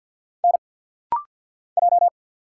Morse code, heard as a keyed tone, about 50 wpm, 700 Hz. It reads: N J
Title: Single Letter-Number Alt - Rapid-Fire 50wpm